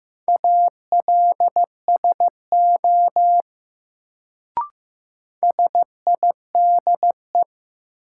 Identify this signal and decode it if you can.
Morse code, heard as a keyed tone, about 15 wpm, 700 Hz. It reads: ALSO SIDE
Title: Sentences from Top 400 Words - New Word Drill - 15wpm